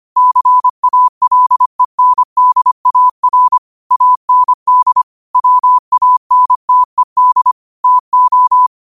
Morse code, heard as a keyed tone, about 25 wpm, 1000 Hz. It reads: CALENDAR AND WANTED TO